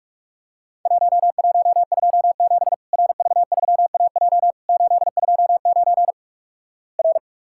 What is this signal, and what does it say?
Morse code, heard as a keyed tone, about 46 wpm, 700 Hz. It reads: 1127 RV3AJ 829 R
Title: Worked All Europe DX Contest - QTC Exchange - 45wpm